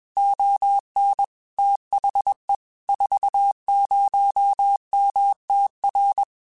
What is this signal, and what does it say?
Morse code, heard as a keyed tone, about 21 wpm, 800 Hz. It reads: ON THE 40MTR
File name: QSOs / QSOs-mp3s / kc7sac@20